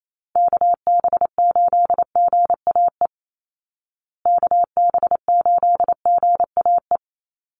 Morse code, heard as a keyed tone, about 28 wpm, 700 Hz. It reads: KB8GAE KB8GAE
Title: Call Signs Repeated 3x - 28wpm